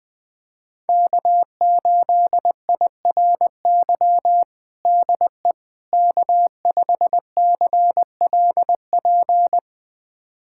Morse code, heard as a keyed tone, about 21 wpm, 700 Hz. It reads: K8IRY DE K5CLP
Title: USA Callsign DE Callsign - 5 Characters - 20wpm